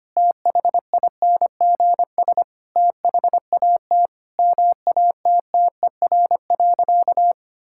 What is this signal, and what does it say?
Morse code, heard as a keyed tone, about 25 wpm, 700 Hz. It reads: THINGS THAT MATTER.
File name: practice / 25wpm / Sentences with 4-Word Phrases - 25wpm